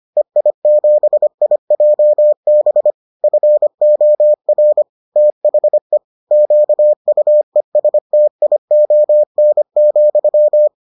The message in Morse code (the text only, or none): EI7IJB FOR THE QUESTION,